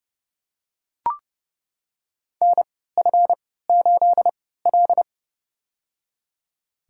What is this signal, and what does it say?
Morse code, heard as a keyed tone, about 31 wpm, spaced as for 17 wpm, 700 Hz. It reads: NF8L
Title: USA Call Signs - 3x Character Spacing - 30wpm